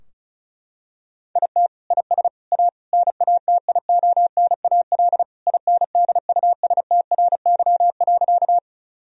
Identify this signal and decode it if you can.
Morse code, heard as a keyed tone, about 34 wpm, 700 Hz. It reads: IT IS A NATIONAL INDUSTRY.
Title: Sentences from Top 1000 Words - Review Part 4 - 35wpm